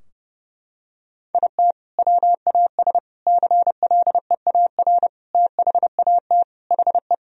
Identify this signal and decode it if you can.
Morse code, heard as a keyed tone, about 30 wpm, 700 Hz. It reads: IT WAS CLEAR THAT HE
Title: Sentences from Top 1000 Words - Review Part 4 - 30wpm